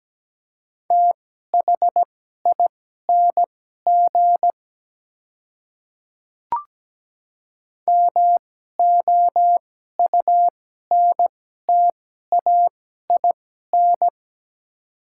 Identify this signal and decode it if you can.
Morse code, heard as a keyed tone, about 17 wpm, spaced as for 12 wpm, 700 Hz. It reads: THING MOUNTAIN
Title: 2x Character Spacing - Top 300 Words - 17wpm